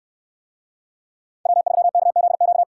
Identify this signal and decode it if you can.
Morse code, heard as a keyed tone, about 64 wpm, 700 Hz. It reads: W3KZQ